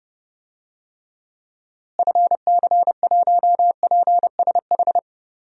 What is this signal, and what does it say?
Morse code, heard as a keyed tone, about 30 wpm, 700 Hz. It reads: FC1PSH
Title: DXCC-100 Call Signs - 6 Characters - Repeated 2x - 30wpm